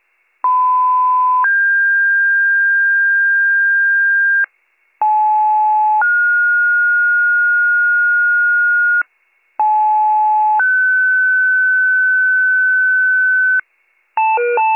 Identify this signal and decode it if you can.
Two-tone page, heard as a one-second tone, then a three-second tone, about 1010 then 1600 Hz, about 860 then 1430 Hz, about 860 then 1500 Hz. It is heard from 0.4 to 13.6 s.